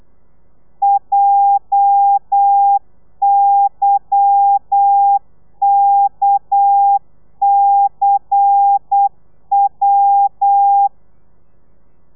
Morse code, heard as a keyed tone, about 8 wpm, 800 Hz. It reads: JYKCW
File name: telegrafia / i2tfj / Lez10 35